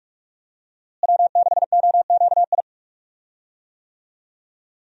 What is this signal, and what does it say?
Morse code, heard as a keyed tone, about 46 wpm, 700 Hz. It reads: W6OQI
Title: USA Call Signs - 5 Characters - Repeated 3x - 45wpm